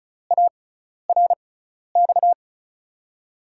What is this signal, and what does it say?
Morse code, heard as a keyed tone, about 35 wpm, spaced as for 12 wpm, 700 Hz. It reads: ARX
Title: USA Call Signs - 6 Characters - 6x Character Spacing - 35wpm